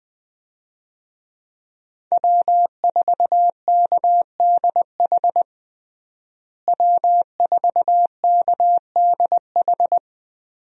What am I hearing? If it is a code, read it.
Morse code, heard as a keyed tone, about 21 wpm, 700 Hz. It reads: W4KDH W4KDH
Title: USA Call Signs - 5 Characters - Repeated 2x - 20wpm